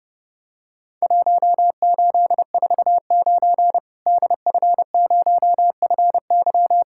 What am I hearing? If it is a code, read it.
Morse code, heard as a keyed tone, about 30 wpm, 700 Hz. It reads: 1849 DF0FY
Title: Worked All Europe DX Contest - QTC Exchange - 30wpm